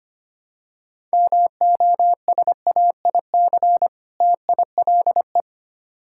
Morse code, heard as a keyed tone, about 25 wpm, 700 Hz. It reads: MOSAIC TILE
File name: practice / 25wpm / Difficult Words - Speed-Racing with Hints - 25wpm